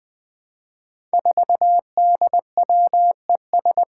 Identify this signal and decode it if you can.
Morse code, heard as a keyed tone, about 20 wpm, 700 Hz. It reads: 4DWES
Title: Five Challenging Letter-Number Combinations 20wpm